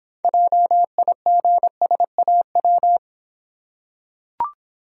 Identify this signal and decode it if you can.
Morse code, heard as a keyed tone, about 26 wpm, 700 Hz. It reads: JIGSAW